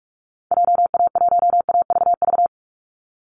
Morse code, heard as a keyed tone, about 44 wpm, 700 Hz. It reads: JA1AVV
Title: USA Call Signs 45wpm